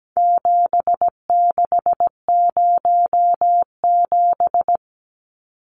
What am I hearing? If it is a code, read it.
Morse code, heard as a keyed tone, about 17 wpm, 700 Hz. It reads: 7607
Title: Numbers - 4-Digit 17wpm